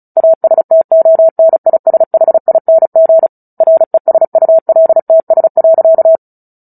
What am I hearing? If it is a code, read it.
Morse code, heard as a keyed tone, about 35 wpm, 650 Hz. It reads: ASTONISHING RESULTS.
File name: Book Two / 09-Chapter 9 - 35 WPM